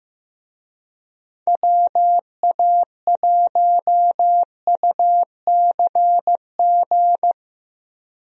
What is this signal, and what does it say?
Morse code, heard as a keyed tone, about 15 wpm, 700 Hz. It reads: WA1UCG
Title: USA Call Signs - 6 Characters - Repeated 2x - 15wpm